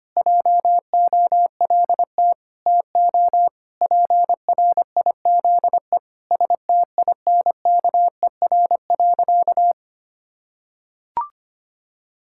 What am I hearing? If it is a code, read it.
Morse code, heard as a keyed tone, about 25 wpm, 700 Hz. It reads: JOLT TO PRIZE STINKER.
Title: Pangram Challenge v2 - 25wpm